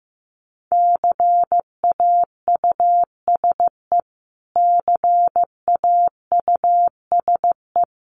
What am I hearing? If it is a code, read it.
Morse code, heard as a keyed tone, about 15 wpm, 700 Hz. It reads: CAUSE CAUSE
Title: Top 500 Words Repeated 15wpm